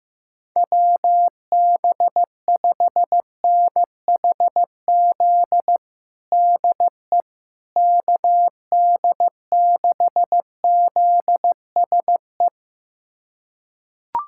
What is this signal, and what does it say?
Morse code, heard as a keyed tone, about 16 wpm, 700 Hz. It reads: WB5NHZ DE KD6ZSE
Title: USA Callsign DE Callsign 15wpm